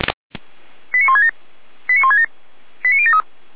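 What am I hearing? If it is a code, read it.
Five-tone selective call (ZVEI), heard as quick runs of five tones, tones of about 70 ms. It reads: 89167 89167 89073
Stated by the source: ZVEI five-tone selective call received on SDR, feuerwehr-orlamuende fire brigade.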